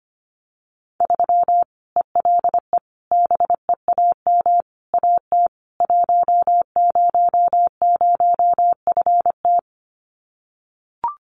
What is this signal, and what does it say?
Morse code, heard as a keyed tone, about 25 wpm, 700 Hz. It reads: 3 ELE BEAM AT 100FT
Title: Common QSO Phrases 25wpm